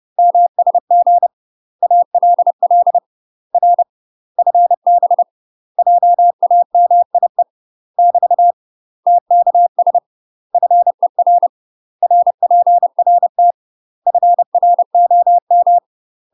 Morse code, heard as a keyed tone, about 30 wpm, 700 Hz. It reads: MSG ALL R FB JAMIE <BT> TKS FER RPRT FROM
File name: UK MASTER QSOs TRAINING / 30WPM / UK MASTER QSOs 0038